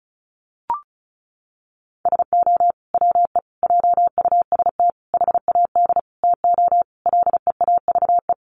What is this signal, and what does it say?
Morse code, heard as a keyed tone, about 34 wpm, 700 Hz. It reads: SO WE JUST HAD TO LEAVE
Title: Sentences from Top 1500 Words 35wpm